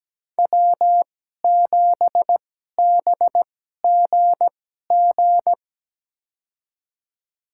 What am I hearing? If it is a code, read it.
Morse code, heard as a keyed tone, about 17 wpm, spaced as for 12 wpm, 700 Hz. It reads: W7BGG
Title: USA Call Signs - 2x Character Spacing - 17wpm